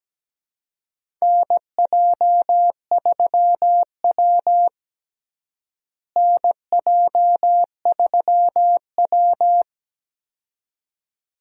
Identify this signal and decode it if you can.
Morse code, heard as a keyed tone, about 17 wpm, 700 Hz. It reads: NJ3W NJ3W
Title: USA Call Signs - 4 Characters - Repeated 2x - 17wpm